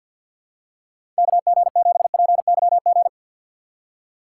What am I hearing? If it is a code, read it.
Morse code, heard as a keyed tone, about 53 wpm, 700 Hz. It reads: KG7PYG
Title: USA Call Signs - 3x Character Spacing - 50wpm